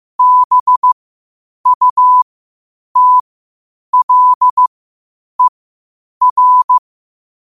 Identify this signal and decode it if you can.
Morse code, heard as a keyed tone, about 15 wpm, spaced as for 8 wpm, 1000 Hz. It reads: BUTLER
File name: qotdfiles / 8wpm